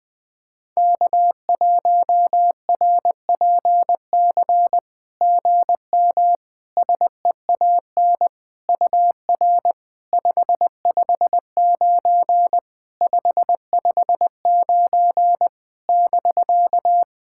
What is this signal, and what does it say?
Morse code, heard as a keyed tone, about 21 wpm, 700 Hz. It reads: K1RPC GM SEAN UR 559 559 <BK>
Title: SOTA QSO Practice 20wpm